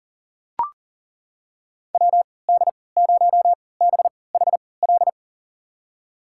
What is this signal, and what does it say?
Morse code, heard as a keyed tone, about 40 wpm, spaced as for 23 wpm, 700 Hz. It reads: WD0BHL
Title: USA Call Signs - 6 Characters - 3x Character Spacing - 40wpm